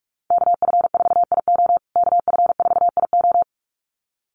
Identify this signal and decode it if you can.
Morse code, heard as a keyed tone, about 46 wpm, 700 Hz. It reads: KF4IO KF4IO
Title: USA Call Signs Repeated 2x 45wpm